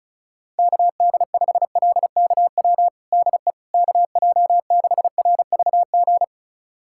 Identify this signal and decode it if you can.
Morse code, heard as a keyed tone, about 34 wpm, 700 Hz. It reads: KD5LKW DE KJ6RVG
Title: USA Callsign DE Callsign - 6 Characters - 35wpm